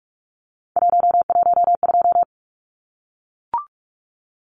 Morse code, heard as a keyed tone, about 44 wpm, 700 Hz. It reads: 112
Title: Numbers - 3-Digit 45wpm